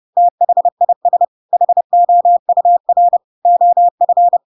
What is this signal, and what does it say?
Morse code, heard as a keyed tone, about 30 wpm, 700 Hz. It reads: THIS HOUR OF
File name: Pinocchio / Pinocchio ch6 30 30 ik1hge